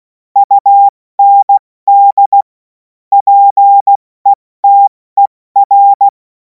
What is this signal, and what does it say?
Morse code, heard as a keyed tone, about 16 wpm, spaced as for 14 wpm, 800 Hz. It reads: UND PETER
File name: Morsekurs / Witze / Witz 07. April 2026 70 F80